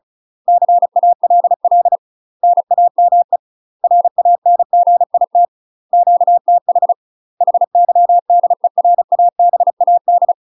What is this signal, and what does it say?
Morse code, heard as a keyed tone, about 34 wpm, 700 Hz. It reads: CALL NAME RANGIT QTH HYDERABAD
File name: UK MASTER QSOs TRAINING / 35WPM / UK MASTER QSOs 0000